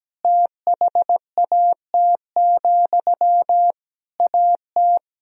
Morse code, heard as a keyed tone, about 17 wpm, 700 Hz. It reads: THAT, AT
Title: Sentences from Top 1000 Words - Review Part 4 - 17wpm